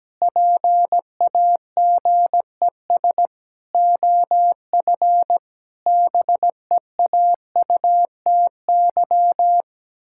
Morse code, heard as a keyed tone, about 17 wpm, 700 Hz. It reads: PAGES OF BEAUTY
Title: Pangram Challenge v2 - 17wpm